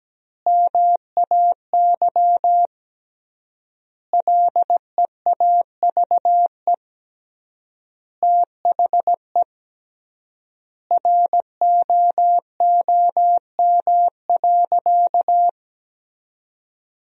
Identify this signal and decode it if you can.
Morse code, heard as a keyed tone, about 17 wpm, 700 Hz. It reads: MAY LEAVE THE ROOM.
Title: 3x Word-Spacing - Sentences from Top 400 Words - 17wpm